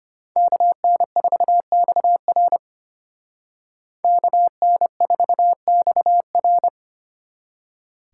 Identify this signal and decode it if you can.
Morse code, heard as a keyed tone, about 33 wpm, 700 Hz. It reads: KN4XR KN4XR